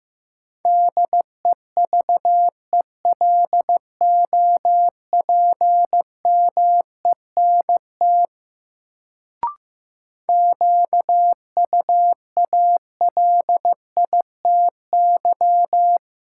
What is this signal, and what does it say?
Morse code, heard as a keyed tone, about 15 wpm, 700 Hz. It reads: DEVELOPMENT QUALITY
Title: Sentences from Top 1000 Words - Review Part 4 - New Word Drill - 15wpm